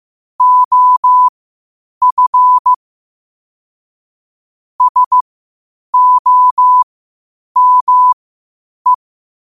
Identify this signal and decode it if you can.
Morse code, heard as a keyed tone, about 15 wpm, spaced as for 8 wpm, 1000 Hz. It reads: OF SOME
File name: qotdfiles / 8wpm